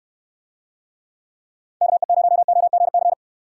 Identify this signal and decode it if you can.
Morse code, heard as a keyed tone, about 67 wpm, 700 Hz. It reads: KE0OGC